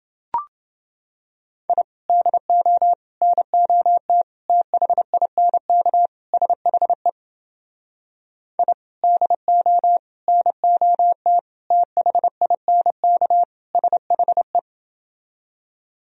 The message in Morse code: I DO NOT THINK SHE I DO NOT THINK SHE

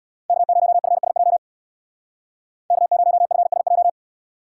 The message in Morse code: K0FSJ K0FSJ

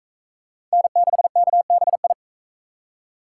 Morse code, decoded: N6KBI

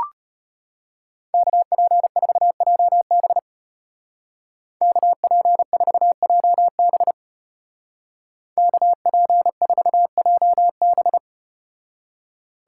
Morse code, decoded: KP4JB KP4JB KP4JB

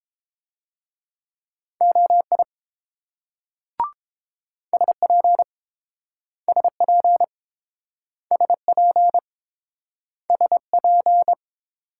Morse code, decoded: OI SP SP SP SP

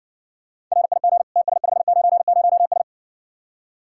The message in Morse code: RIG TS590S

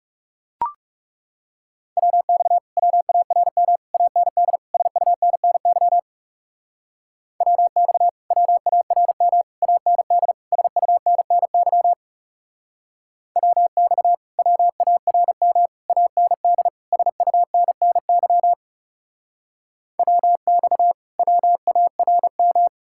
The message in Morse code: WX WARM AND SUNNY WX WARM AND SUNNY WX WARM AND SUNNY WX WARM

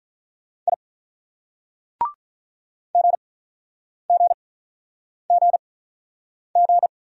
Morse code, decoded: I GGGG

